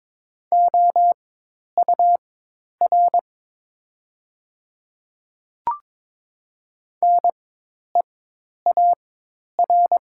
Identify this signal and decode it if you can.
Morse code, heard as a keyed tone, about 22 wpm, spaced as for 10 wpm, 700 Hz. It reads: OUR NEAR
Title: 4x Character Spacing - Top 200 Words - 22wpm